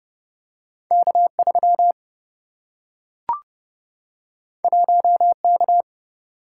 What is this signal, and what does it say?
Morse code, heard as a keyed tone, about 29 wpm, 700 Hz. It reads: K3 1K
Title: Single-Letter Single-Number 30wpm